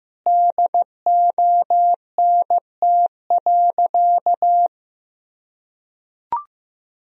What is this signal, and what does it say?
Morse code, heard as a keyed tone, about 15 wpm, 700 Hz. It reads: DONT.
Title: Sentences from Top 300 Words and Abbreviations 15wpm